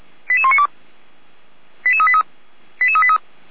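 Five-tone selective call (ZVEI), heard as quick runs of five tones, tones of about 70 ms. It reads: 88192 88383 88383
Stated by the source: ZVEI five-tone call 88102 received by SDR, from feuerwehr-orlamuende fire brigade server.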